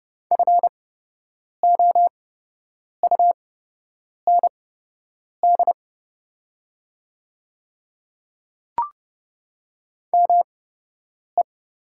Morse code, heard as a keyed tone, about 30 wpm, spaced as for 8 wpm, 700 Hz. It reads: FOUND ME
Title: 8x Character Spacing - Top 500 Words - 30wpm